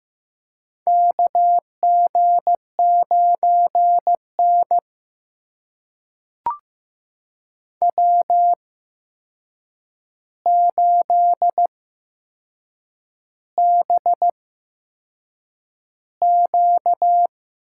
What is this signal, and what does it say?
Morse code, heard as a keyed tone, about 15 wpm, spaced as for 6 wpm, 700 Hz. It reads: KG9N W8BQ
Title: USA Call Signs - 4 Characters - 8x Character Spacing - 15wpm